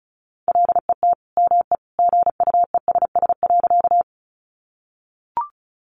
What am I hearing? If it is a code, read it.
Morse code, heard as a keyed tone, about 34 wpm, 700 Hz. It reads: LET ME GUESS.
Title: Sentences from Top 1500 Words 35wpm